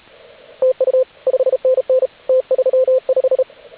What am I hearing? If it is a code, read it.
Morse code, heard as a keyed tone, about 33 wpm, 520 Hz, with radio noise. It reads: TU 5NN T35
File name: AUDIO2019 / HG8DX CW